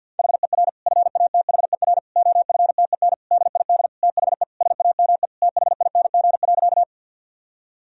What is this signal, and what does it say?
Morse code, heard as a keyed tone, about 51 wpm, 700 Hz. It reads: HER FATHER OFTEN DID THE SAME THING.
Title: Sentences from Top 1000 Words - Review Part 1 - 50wpm